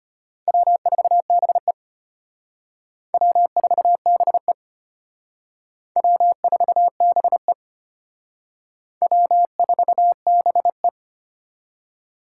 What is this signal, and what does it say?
Morse code, heard as a keyed tone, about 31 wpm, 700 Hz. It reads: W4BE W4BE W4BE W4BE